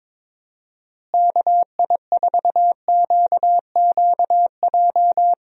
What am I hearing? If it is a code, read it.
Morse code, heard as a keyed tone, about 22 wpm, 700 Hz. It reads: KI4QQJ
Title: USA Call Signs - 6 Characters - Repeated 3x - 22wpm